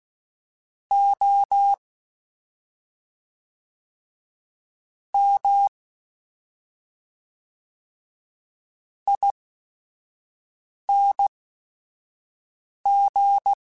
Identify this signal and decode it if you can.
Morse code, heard as a keyed tone, about 16 wpm, spaced as for 5 wpm, 800 Hz. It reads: O M ING
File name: Special INGs / Special INGs-mp3s / SlowToFaster ing @5